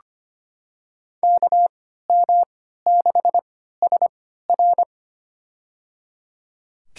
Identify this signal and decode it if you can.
Morse code, heard as a keyed tone, about 25 wpm, spaced as for 14 wpm, 700 Hz. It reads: KM6SR
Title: USA Call Signs - 5 Characters - 3x Character Spacing - 25wpm